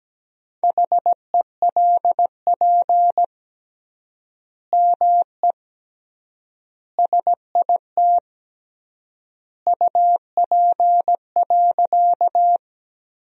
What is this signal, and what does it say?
Morse code, heard as a keyed tone, about 17 wpm, 700 Hz. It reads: HELP ME SIT UP.
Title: 3x Word-Spacing - Sentences from Top 500 Words - 17wpm